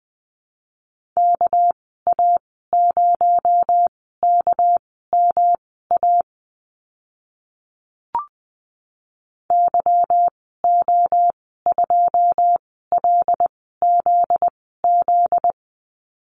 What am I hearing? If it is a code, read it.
Morse code, heard as a keyed tone, about 20 wpm, spaced as for 14 wpm, 700 Hz. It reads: KA0KMA YO2LZZ
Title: 245 Lesson 39g - Letter z - Call Signs 20wpm